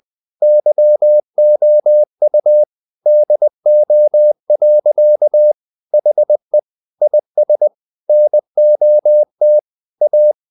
Morse code, heard as a keyed tone, about 20 wpm, 600 Hz. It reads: YOU DO. HE IS NOT A
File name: the old man and the sea part 2 20wpm